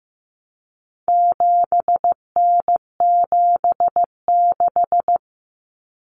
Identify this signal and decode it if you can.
Morse code, heard as a keyed tone, about 15 wpm, 700 Hz. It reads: 7N76
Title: Common Cut Numbers - 4-Digit 15wpm